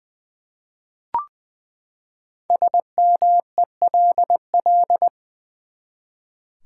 Morse code, heard as a keyed tone, about 20 wpm, 700 Hz. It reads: SMELL
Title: Five Letter Words - 20wpm